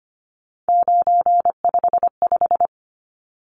Morse code, heard as a keyed tone, about 25 wpm, 700 Hz. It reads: 955
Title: Numbers - 3-Digit 25wpm